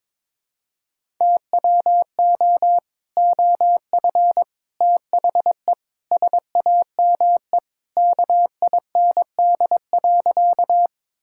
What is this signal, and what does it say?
Morse code, heard as a keyed tone, about 22 wpm, 700 Hz. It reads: TWO OF THE SAME KIND.
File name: practice / 22wpm / Sentences with 3-Word Phrases - 22wpm